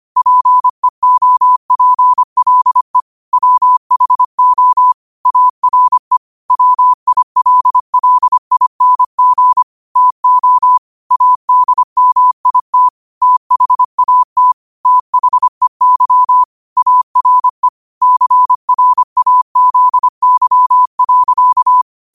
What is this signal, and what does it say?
Morse code, heard as a keyed tone, about 25 wpm, 1000 Hz. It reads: PEOPLE WHO ARE WILLING TO ADMIT THAT THEY ARE CRAZY.
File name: qotdfiles / 25wpm